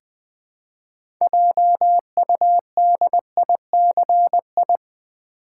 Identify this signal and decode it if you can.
Morse code, heard as a keyed tone, about 20 wpm, 700 Hz. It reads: JUDICI
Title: Build A Word - 2 Letters At A Time - 8-Letter Words - 20wpm